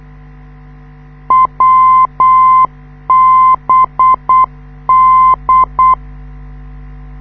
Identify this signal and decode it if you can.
Morse code, heard as a keyed tone, about 8 wpm, 1040 Hz. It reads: WBD
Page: this is how the NDB of the Wiesbaden US Army Airfield at Erbenheim sounds like, when you live 2 km away